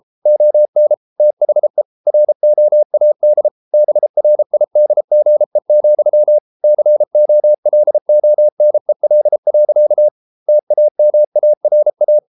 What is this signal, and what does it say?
Morse code, heard as a keyed tone, about 33 wpm, 600 Hz. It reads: ON THE ROAD BRIDGE, COLONEL. TAMARA